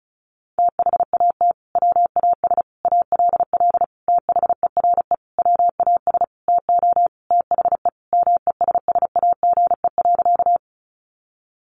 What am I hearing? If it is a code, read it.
Morse code, heard as a keyed tone, about 34 wpm, 700 Hz. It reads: THAT WAS ALL THERE WAS TO THE MESSAGE.
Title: Sentences from Top 1100 Words 35wpm